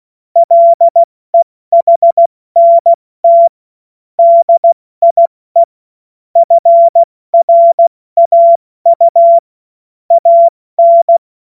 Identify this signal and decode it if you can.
Morse code, heard as a keyed tone, about 16 wpm, 680 Hz. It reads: LEHNT DIE FRAU AN
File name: Morsekurs / Witze / Witz 13. Dezember 2022 70 F80